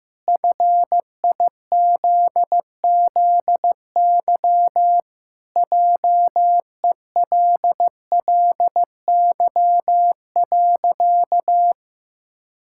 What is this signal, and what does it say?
Morse code, heard as a keyed tone, about 16 wpm, 700 Hz. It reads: FIZZY JELLY.
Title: Pangram Challenge v2 - 15wpm